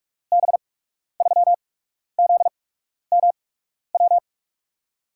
Morse code, heard as a keyed tone, about 44 wpm, spaced as for 12 wpm, 700 Hz. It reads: B3ZMW
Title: USA Call Signs - 6 Characters - 8x Character Spacing - 45wpm